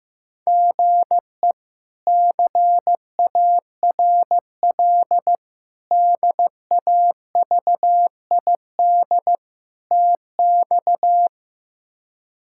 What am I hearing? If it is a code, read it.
Morse code, heard as a keyed tone, about 15 wpm, 700 Hz. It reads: GE CARL DAVID TX
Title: K1USN SST QSO Practice - 15wpm